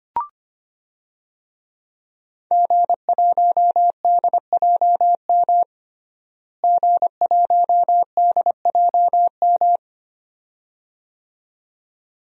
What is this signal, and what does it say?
Morse code, heard as a keyed tone, about 25 wpm, 700 Hz. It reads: G1DJM G1DJM
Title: DXCC-100 Call Signs - 5 Characters - Repeated 2x - 25wpm